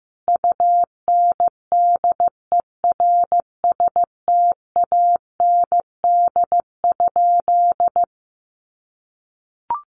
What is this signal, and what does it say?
Morse code, heard as a keyed tone, about 15 wpm, 700 Hz. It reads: UNDERSTAND?
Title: Sentences from Top 600 Words 15wpm